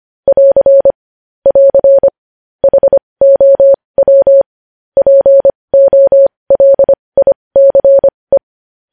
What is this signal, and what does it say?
Morse code, heard as a keyed tone, about 25 wpm, spaced as for 20 wpm, 550 Hz. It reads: <AR> <AR> HOW POLICE
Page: News Headlines in Morse Code at 20 WPM